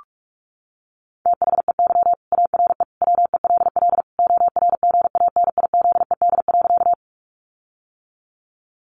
Morse code, heard as a keyed tone, about 44 wpm, 700 Hz. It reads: THEY ARE WELL ORGANIZED.
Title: Sentences from Top 1300 Words 45wpm